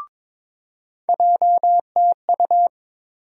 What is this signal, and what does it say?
Morse code, heard as a keyed tone, about 22 wpm, 700 Hz. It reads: JTU
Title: USA Call Sign Postfixes - 3 Letters 22wpm